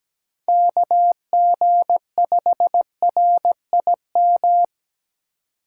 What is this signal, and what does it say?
Morse code, heard as a keyed tone, about 17 wpm, 700 Hz. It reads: KG5RIM
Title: USA Call Signs - 6 Characters - 3x Character Spacing - 17wpm